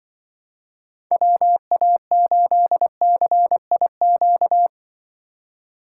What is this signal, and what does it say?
Morse code, heard as a keyed tone, about 24 wpm, 700 Hz. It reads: WA8CIQ
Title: Speed Racing - USA Call Signs 20wpm